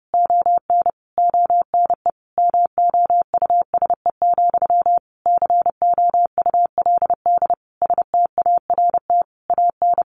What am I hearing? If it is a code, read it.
Morse code, heard as a keyed tone, about 30 wpm, 700 Hz. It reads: ON ONE MOUSE, COULD START AN
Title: A Sound Of Thunder 30wpm